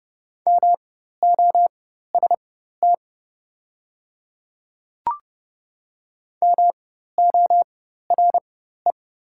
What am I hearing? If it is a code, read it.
Morse code, heard as a keyed tone, about 30 wpm, spaced as for 14 wpm, 700 Hz. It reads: MOST MORE
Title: 4x Character Spacing - Top 300 Words - 30wpm